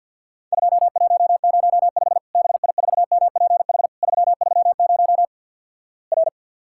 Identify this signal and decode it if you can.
Morse code, heard as a keyed tone, about 50 wpm, 700 Hz. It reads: 2105 BI4MPH 330 R
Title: Worked All Europe DX Contest - QTC Exchange with Repeats - 50wpm